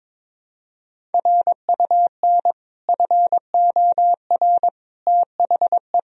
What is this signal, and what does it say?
Morse code, heard as a keyed tone, about 22 wpm, 700 Hz. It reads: RUN FOR THE
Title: Sentences from Top 1000 Words - Review Part 1 - 22wpm